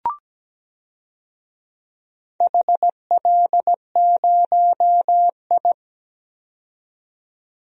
Morse code, heard as a keyed tone, about 17 wpm, 700 Hz. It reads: HL0I
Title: DXCC-100 Call Signs - 4 Characters - 17wpm